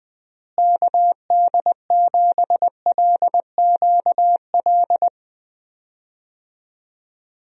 Morse code, heard as a keyed tone, about 20 wpm, 700 Hz. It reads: KD7LQL
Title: USA Call Signs - 6 Characters - Repeated 2x - 20wpm